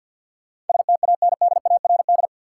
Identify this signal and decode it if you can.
Morse code, heard as a keyed tone, about 51 wpm, 700 Hz. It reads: STANDARD
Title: Sentences from Top 1200 Words - New Word Drill Alt - 50wpm